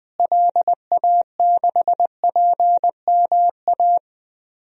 Morse code, heard as a keyed tone, about 20 wpm, 700 Hz. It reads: LA6PMA
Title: DXCC-100 Call Signs - 6 Characters - Repeated 2x - 20wpm